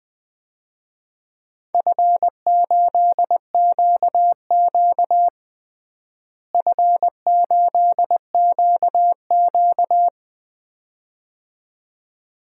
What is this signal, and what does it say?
Morse code, heard as a keyed tone, about 20 wpm, 700 Hz. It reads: F8QQ F8QQ
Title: DXCC-100 Call Signs - 4 Characters - Repeated 2x - 20wpm